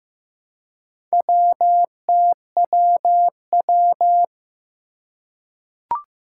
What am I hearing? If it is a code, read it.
Morse code, heard as a keyed tone, about 15 wpm, 700 Hz. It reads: WTWW
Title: Four Challenging Letter-Number Combinations 15wpm